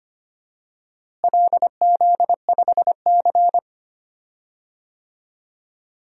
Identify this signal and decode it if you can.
Morse code, heard as a keyed tone, about 25 wpm, 700 Hz. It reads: LZ5C
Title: DXCC-100 Call Signs - 4 Characters - Repeated 3x - 25wpm